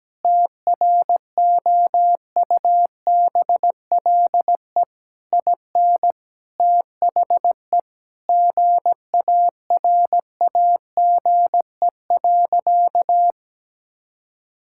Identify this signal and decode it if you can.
Morse code, heard as a keyed tone, about 17 wpm, 700 Hz. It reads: TROUBLE IN THE GARAGE.
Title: Sentences from Top 1600 Words 17wpm